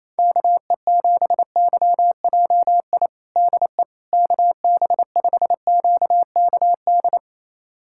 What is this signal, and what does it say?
Morse code, heard as a keyed tone, about 28 wpm, 700 Hz. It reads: KE7YJI DE KB5QKD
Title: USA Callsign DE Callsign - 6 Characters - 28wpm